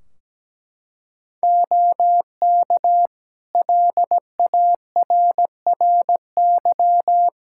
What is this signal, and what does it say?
Morse code, heard as a keyed tone, about 17 wpm, 700 Hz. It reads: OK LARRY
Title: Practice QSO - Exchange 2 - Info - 17wpm